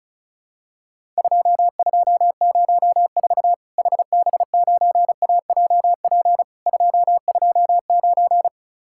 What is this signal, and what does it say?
Morse code, heard as a keyed tone, about 36 wpm, 700 Hz. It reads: 2204 HB9AJP 229
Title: Worked All Europe DX Contest - QTC Exchange - 35wpm